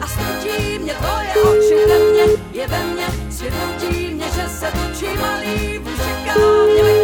Ringback tone (Europe, 425 Hz), with noise.